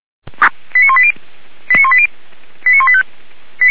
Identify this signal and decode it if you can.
Five-tone selective call (ZVEI), heard as quick runs of five tones, tones of about 70 ms. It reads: 89180 89180 89185
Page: /464/SDR/ZVEI/89131/2021.11.27/